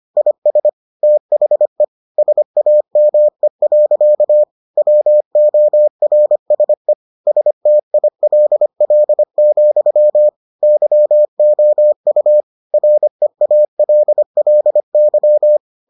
Morse code, heard as a keyed tone, about 25 wpm, 600 Hz. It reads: IS THE SAME. WORSE STILL, YOU REALLY